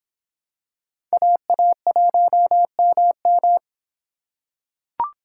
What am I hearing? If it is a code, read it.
Morse code, heard as a keyed tone, about 25 wpm, 700 Hz. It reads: AA1MM